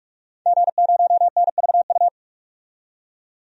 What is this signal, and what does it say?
Morse code, heard as a keyed tone, about 46 wpm, 700 Hz. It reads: G0NVU
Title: DXCC-100 Call Signs - 5 Characters - 45wpm